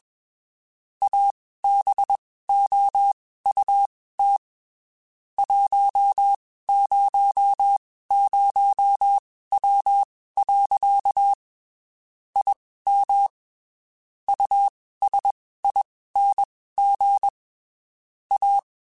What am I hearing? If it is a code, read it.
Morse code, heard as a keyed tone, about 21 wpm, spaced as for 15 wpm, 800 Hz. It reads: ABOUT 100W. IM USING A